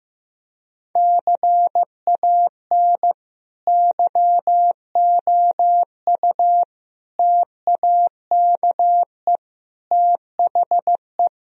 Morse code, heard as a keyed tone, about 16 wpm, 700 Hz. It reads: CAN YOU TAKE THE
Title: Sentences from Top 1000 Words - Review Part 3 - 15wpm